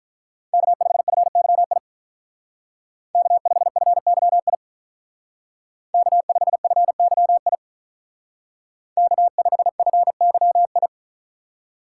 Morse code, heard as a keyed tone, about 46 wpm, 700 Hz. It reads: K5FYI K5FYI K5FYI K5FYI